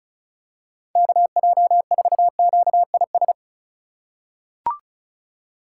Morse code, heard as a keyed tone, about 36 wpm, 700 Hz. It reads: KJ4QIS
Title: USA Call Signs - 6 Characters - 4x Character Spacing - 35wpm